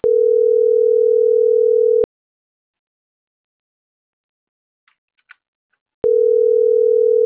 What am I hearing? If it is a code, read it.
Ringback tone (North America), 440 + 480 Hz.